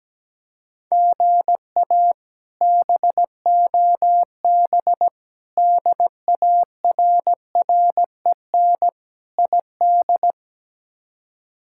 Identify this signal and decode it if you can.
Morse code, heard as a keyed tone, about 17 wpm, 700 Hz. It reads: GA BOB DARREN ID
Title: K1USN SST QSO Practice - 17wpm